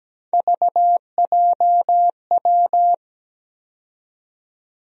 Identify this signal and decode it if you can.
Morse code, heard as a keyed tone, about 17 wpm, 700 Hz. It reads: VJW
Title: USA Call Sign Postfixes - 3 Letters 17wpm